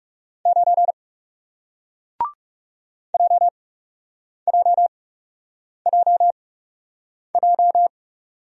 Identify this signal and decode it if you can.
Morse code, heard as a keyed tone, about 39 wpm, spaced as for 8 wpm, 700 Hz. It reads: 9 JJJJ